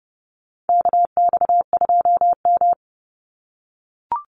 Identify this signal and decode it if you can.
Morse code, heard as a keyed tone, about 30 wpm, 700 Hz. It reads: KX2M
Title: USA Call Signs 30wpm